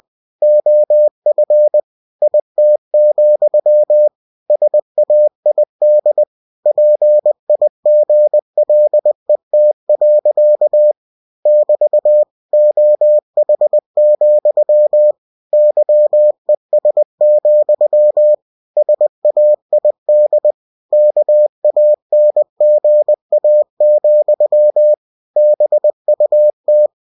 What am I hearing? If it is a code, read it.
Morse code, heard as a keyed tone, about 20 wpm, 600 Hz. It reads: OF IT, SAID PIGLET. <BT> OH, YES, SAID KANGA, BUT